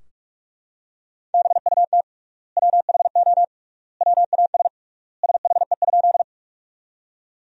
Morse code, heard as a keyed tone, about 44 wpm, 700 Hz. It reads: BUT WHO WAS SHE?
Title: 3x Word-Spacing - Sentences from Top 100 Words - 45wpm